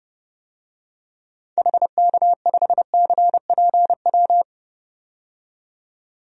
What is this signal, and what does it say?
Morse code, heard as a keyed tone, about 30 wpm, 700 Hz. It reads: HK5CPW
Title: DXCC-100 Call Signs - 6 Characters - 30wpm